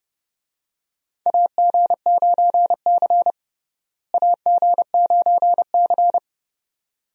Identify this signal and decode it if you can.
Morse code, heard as a keyed tone, about 29 wpm, 700 Hz. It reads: AG9C AG9C
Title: USA Call Signs - 4 Characters - Repeated 2x - 30wpm